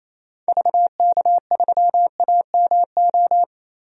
Morse code, heard as a keyed tone, about 28 wpm, 700 Hz. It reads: VK3AMO
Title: DXCC-100 Call Signs - 6 Characters - Repeated 3x - 28wpm